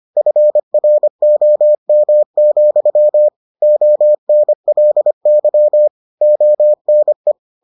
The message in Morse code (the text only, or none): FROM, ONLY ONE